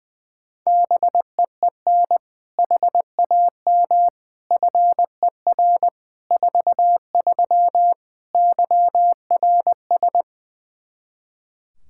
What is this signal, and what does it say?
Morse code, heard as a keyed tone, about 20 wpm, 700 Hz. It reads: BEEN HAM FER 43 YRS
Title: Practice QSO - Exchange 3 - More Info - 20wpm